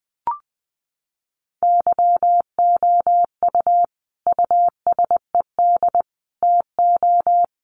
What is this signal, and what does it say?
Morse code, heard as a keyed tone, about 20 wpm, 700 Hz. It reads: YOU USED TO
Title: Sentences from Top 600 Words 20wpm